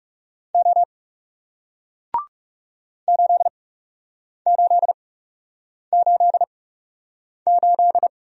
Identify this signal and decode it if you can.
Morse code, heard as a keyed tone, about 40 wpm, spaced as for 8 wpm, 700 Hz. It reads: O 8888